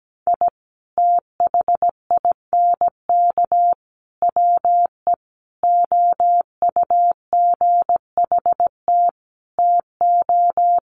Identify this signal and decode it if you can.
Morse code, heard as a keyed tone, about 17 wpm, 700 Hz. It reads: I THINK WE OUGHT TO
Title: Sets of 5 Words Encore 17wpm